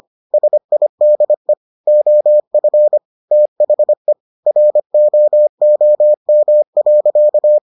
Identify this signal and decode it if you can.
Morse code, heard as a keyed tone, about 25 wpm, 600 Hz. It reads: SIDE OF THE ROOM.